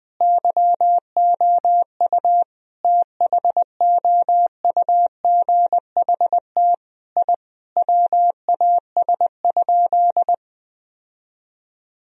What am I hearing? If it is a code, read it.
Morse code, heard as a keyed tone, about 20 wpm, 700 Hz. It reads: YOU THOUGHT I WAS?
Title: Sentences from Top 1000 Words - Review Part 3 - 20wpm